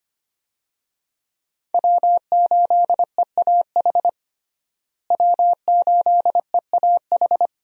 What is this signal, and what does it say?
Morse code, heard as a keyed tone, about 25 wpm, 700 Hz. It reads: W8EAH W8EAH
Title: USA Call Signs - 5 Characters - Repeated 2x - 25wpm